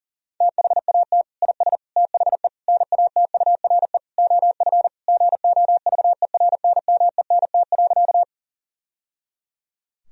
Morse code, heard as a keyed tone, about 40 wpm, 700 Hz. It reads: THAT IS THE NATURE OF GOVERNMENT.
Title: Sentences from Top 1000 Words - Review Part 3 - 40wpm